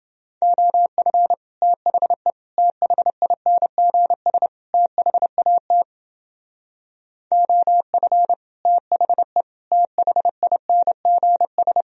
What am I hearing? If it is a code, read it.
Morse code, heard as a keyed tone, about 29 wpm, 700 Hz. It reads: OF THE THINGS THAT OF THE THINGS